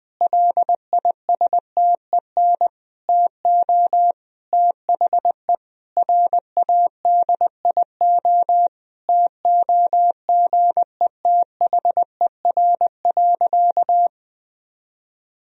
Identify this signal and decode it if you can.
Morse code, heard as a keyed tone, about 20 wpm, 700 Hz. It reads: LISTEN TO THE RADIO TOGETHER.
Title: Sentences from Top 1000 Words - Review Part 3 - 20wpm